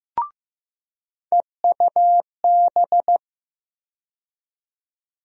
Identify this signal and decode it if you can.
Morse code, heard as a keyed tone, about 15 wpm, 700 Hz. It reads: EUB
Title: USA Call Sign Postfixes - 3 Letters 15wpm